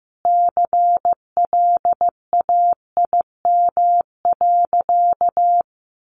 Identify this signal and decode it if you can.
Morse code, heard as a keyed tone, about 15 wpm, 700 Hz. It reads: CLAIM.
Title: Sentences from Top 1500 Words 15wpm